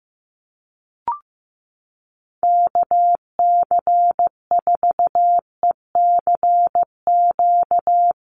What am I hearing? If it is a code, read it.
Morse code, heard as a keyed tone, about 15 wpm, 700 Hz. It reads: KC4ECQ
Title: USA Call Signs 15wpm